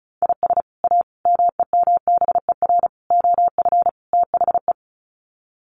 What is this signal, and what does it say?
Morse code, heard as a keyed tone, about 34 wpm, 700 Hz. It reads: IS A MEMBER OF THE
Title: Sets of 5 Words 35wpm